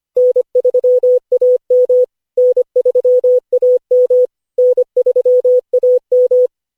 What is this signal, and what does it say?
Morse code, heard as a keyed tone, about 25 wpm, 500 Hz. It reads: N3AM N3AM N3AM
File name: cw / Session5 cw training